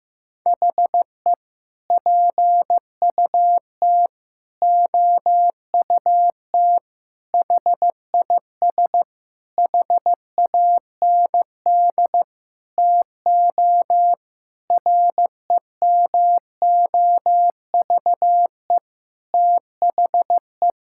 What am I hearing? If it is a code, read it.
Morse code, heard as a keyed tone, about 14 wpm, 700 Hz. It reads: HE PUT OUT HIS HAND TO REMOVE THE
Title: Sentences from Top 1000 Words - Review Part 5 - 15wpm